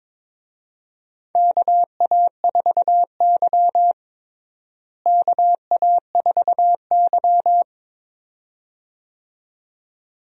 Morse code, heard as a keyed tone, about 22 wpm, 700 Hz. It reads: KA4Y KA4Y
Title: USA Call Signs - 4 Characters - Repeated 2x - 22wpm